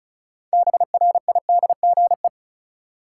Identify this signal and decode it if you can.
Morse code, heard as a keyed tone, about 36 wpm, 700 Hz. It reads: BRIDGE
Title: Sentences from Top 1200 Words - New Word Drill - 35wpm